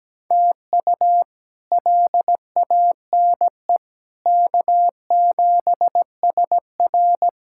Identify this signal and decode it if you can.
Morse code, heard as a keyed tone, about 17 wpm, 700 Hz. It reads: TU LANE K7SR
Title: K1USN SST QSO Practice - 17wpm